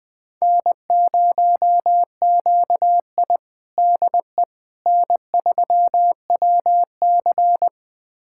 Morse code, heard as a keyed tone, about 21 wpm, 700 Hz. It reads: N0QI DE N3WC
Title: USA Callsign DE Callsign - 4 Characters - 20wpm